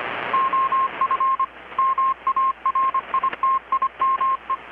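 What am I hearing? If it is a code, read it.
Morse code, heard as a keyed tone, about 25 wpm, 1060 Hz, with radio noise. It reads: OF MARITIME
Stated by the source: Morse from KPH's Night of Nights XX maritime radiotelegraphy event, received in Chiapas, Mexico about 3,800 km away on a Kenwood R-600 with a grounded 10 m long-wire antenna.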